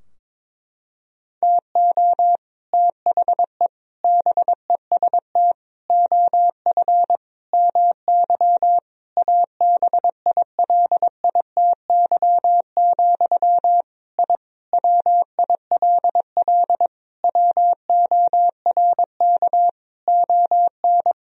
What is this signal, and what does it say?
Morse code, heard as a keyed tone, about 22 wpm, 700 Hz. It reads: TO THE BEST OF MY ABILITY, I WILL WORK ON
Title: Sentences from Top 1000 Words - Review Part 3 - 22wpm